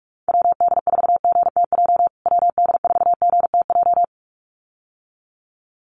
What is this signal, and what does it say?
Morse code, heard as a keyed tone, about 44 wpm, 700 Hz. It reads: WD4GTJ WD4GTJ
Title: USA Call Signs Repeated 2x 45wpm